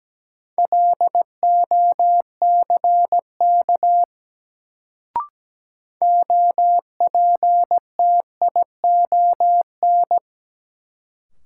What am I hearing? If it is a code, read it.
Morse code, heard as a keyed tone, about 17 wpm, 700 Hz. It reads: LOCK OPTION
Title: Sentences from Top 1300 Words - New Word Drill Alt - 17wpm